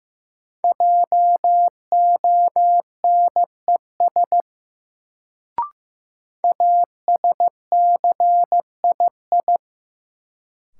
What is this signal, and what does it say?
Morse code, heard as a keyed tone, about 15 wpm, 700 Hz. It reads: JONES ASCII
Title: Sets of 5 Words - New Word Drill - 15wpm